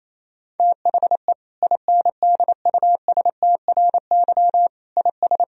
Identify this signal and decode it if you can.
Morse code, heard as a keyed tone, about 28 wpm, 700 Hz. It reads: THE INDUSTRY IS
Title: Sentences from Top 1000 Words - Review Part 4 - 28wpm